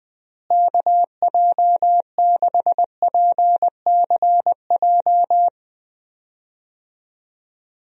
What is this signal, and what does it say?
Morse code, heard as a keyed tone, about 20 wpm, 700 Hz. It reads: KJ6PCJ
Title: USA Call Signs - 6 Characters - Repeated 2x - 20wpm